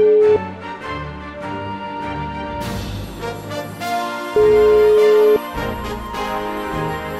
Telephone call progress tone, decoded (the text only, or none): ringback tone